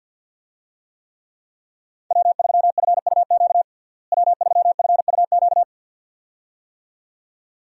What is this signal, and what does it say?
Morse code, heard as a keyed tone, about 52 wpm, 700 Hz. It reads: W3FUQ W3FUQ
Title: USA Call Signs - 5 Characters - Repeated 2x - 50wpm